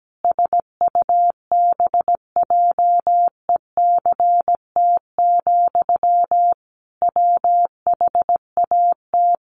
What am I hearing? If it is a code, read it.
Morse code, heard as a keyed tone, about 17 wpm, 700 Hz. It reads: SUBJECT, WHAT
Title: Sentences from Top 800 Words 17wpm